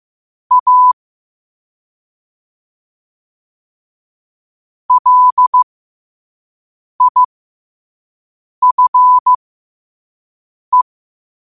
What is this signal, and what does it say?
Morse code, heard as a keyed tone, about 15 wpm, spaced as for 5 wpm, 1000 Hz. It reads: A LIFE
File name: qotdfiles / 5wpm